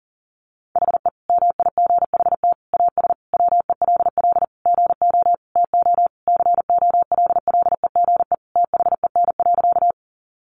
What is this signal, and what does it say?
Morse code, heard as a keyed tone, about 40 wpm, 700 Hz. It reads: HE MIGHT AS WELL GO TO COLLEGE THEN.
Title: Sentences from Top 1300 Words 40wpm